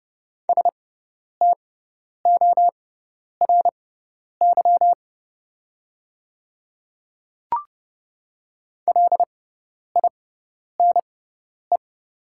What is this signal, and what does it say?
Morse code, heard as a keyed tone, about 30 wpm, spaced as for 10 wpm, 700 Hz. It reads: STORY LINE
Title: 6x Character Spacing - Top 300 Words - 30wpm